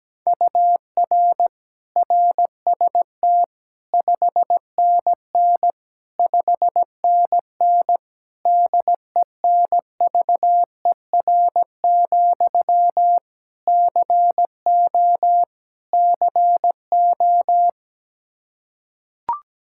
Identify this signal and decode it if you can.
Morse code, heard as a keyed tone, about 17 wpm, 700 Hz. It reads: UR RST 5NN 5NN DENVER, CO CO
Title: Practice QSO - Exchange 1 - Beginning - 17wpm